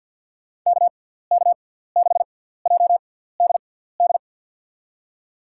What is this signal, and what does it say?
Morse code, heard as a keyed tone, about 48 wpm, spaced as for 17 wpm, 700 Hz. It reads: KK6JDD
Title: USA Call Signs - 6 Characters - 6x Character Spacing - 50wpm